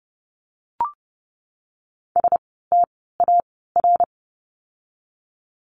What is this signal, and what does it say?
Morse code, heard as a keyed tone, about 30 wpm, spaced as for 17 wpm, 700 Hz. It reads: STAR
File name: Lesson 17f - Letter u - Words 30wpm